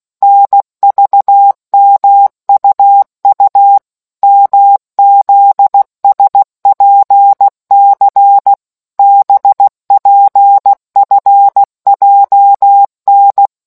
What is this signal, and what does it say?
Morse code, heard as a keyed tone, about 16 wpm, 800 Hz. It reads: NVMUU MZSPC BPFJN